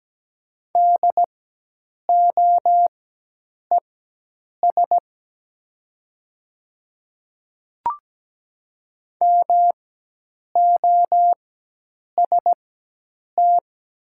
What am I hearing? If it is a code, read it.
Morse code, heard as a keyed tone, about 17 wpm, spaced as for 8 wpm, 700 Hz. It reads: DOES MOST
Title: 4x Character Spacing - Top 300 Words - 17wpm